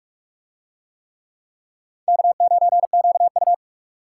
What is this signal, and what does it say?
Morse code, heard as a keyed tone, about 41 wpm, 700 Hz. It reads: K9QU